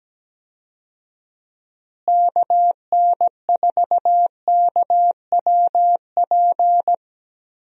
Morse code, heard as a keyed tone, about 17 wpm, 700 Hz. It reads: KN4KWP
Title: USA Call Signs - 6 Characters - Repeated 2x - 17wpm